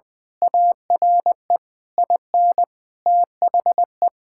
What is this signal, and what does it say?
Morse code, heard as a keyed tone, about 21 wpm, 700 Hz. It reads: ARE IN THE
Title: Sentences from Top 1000 Words - Review Part 5 - 20wpm